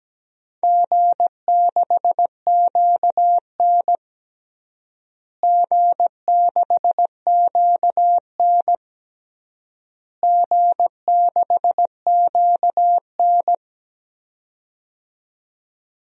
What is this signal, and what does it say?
Morse code, heard as a keyed tone, about 17 wpm, 700 Hz. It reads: G6QN G6QN G6QN
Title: DXCC-100 Call Signs - 4 Characters - Repeated 3x - 17wpm